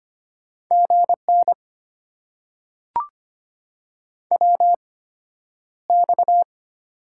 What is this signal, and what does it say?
Morse code, heard as a keyed tone, about 25 wpm, spaced as for 11 wpm, 700 Hz. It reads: GN WX
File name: Top 90 QSO Elements - 8x Character Spacing - 25wpm